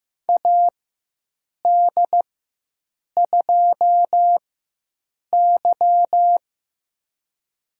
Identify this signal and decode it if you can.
Morse code, heard as a keyed tone, about 15 wpm, spaced as for 7 wpm, 700 Hz. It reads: AD2Y
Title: USA Call Signs - 4 Characters - 4x Character Spacing - 15wpm